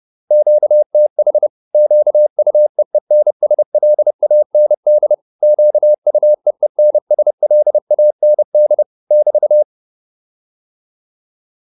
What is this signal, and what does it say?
Morse code, heard as a keyed tone, about 30 wpm, 600 Hz. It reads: QTH QUEENSLAND QUEENSLAND <BT>